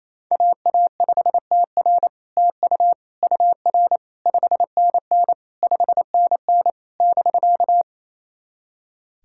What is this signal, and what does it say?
Morse code, heard as a keyed tone, about 27 wpm, 700 Hz. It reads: AA5TR TU UR 5NN 5NN <BK>
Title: SOTA QSO Practice 28wpm